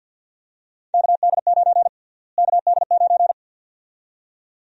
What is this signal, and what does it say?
Morse code, heard as a keyed tone, about 51 wpm, 700 Hz. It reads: KD9 KD9
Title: USA Call Sign Prefixes - Repeated 2x - 50wpm